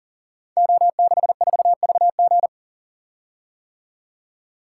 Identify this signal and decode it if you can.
Morse code, heard as a keyed tone, about 42 wpm, 700 Hz. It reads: O64VG
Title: Code Groups - 5 Letter:Numbers - 40wpm